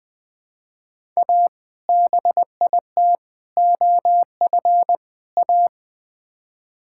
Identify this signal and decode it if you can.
Morse code, heard as a keyed tone, about 20 wpm, 700 Hz. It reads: A BIT OF A
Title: Build A Phrase - 4-Word Phrases - 20wpm